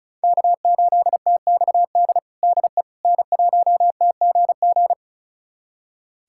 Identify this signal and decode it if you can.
Morse code, heard as a keyed tone, about 36 wpm, 700 Hz. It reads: K8TXD DE N1TGG
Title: USA Callsign DE Callsign - 5 Characters - 35wpm